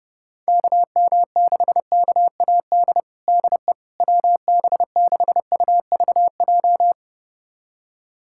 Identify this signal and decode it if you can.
Morse code, heard as a keyed tone, about 32 wpm, 700 Hz. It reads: KM6KAD DE WB6UVJ
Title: USA Callsign DE Callsign - 6 Characters - 30wpm